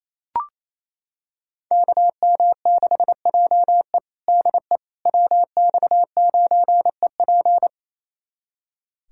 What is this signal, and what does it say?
Morse code, heard as a keyed tone, about 28 wpm, 700 Hz. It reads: KM6JE DE WX9EP
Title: USA Callsign DE Callsign 28wpm